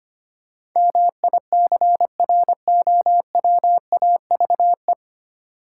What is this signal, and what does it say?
Morse code, heard as a keyed tone, about 25 wpm, 700 Hz. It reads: MICROWAVE
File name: practice / 25wpm / Easy Words with Hints - 25wpm